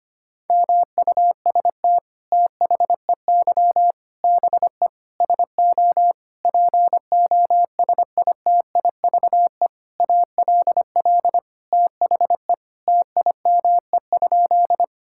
Morse code, heard as a keyed tone, about 25 wpm, 700 Hz. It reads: MUST THEY BE SO POSITIVE ALL THE TIME?
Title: Sentences from Top 1000 Words - Review Part 5 - 25wpm